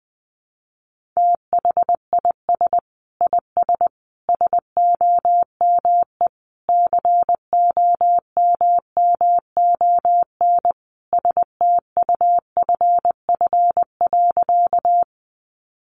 Morse code, heard as a keyed tone, about 20 wpm, 700 Hz. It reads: THIS IS SOME COMMON STUFF.
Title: Sentences from Top 1400 Words 20wpm